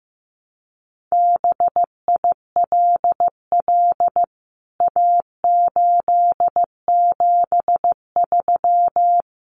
Morse code, heard as a keyed tone, about 15 wpm, 700 Hz. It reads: BILL A873
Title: CWT Exchange 15wpm